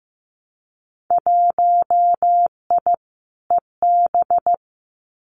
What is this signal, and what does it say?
Morse code, heard as a keyed tone, about 15 wpm, 700 Hz. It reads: 1I EB
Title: Winter Field Day Exchanges Encore 15wpm